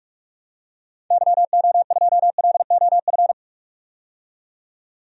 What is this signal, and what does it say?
Morse code, heard as a keyed tone, about 46 wpm, 700 Hz. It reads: YO2LOF
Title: DXCC-100 Call Signs - 6 Characters - 45wpm